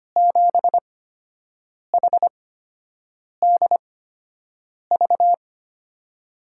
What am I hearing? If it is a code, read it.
Morse code, heard as a keyed tone, about 25 wpm, spaced as for 7 wpm, 700 Hz. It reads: 7HDV
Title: USA Call Signs - 6 Characters - 8x Character Spacing - 25wpm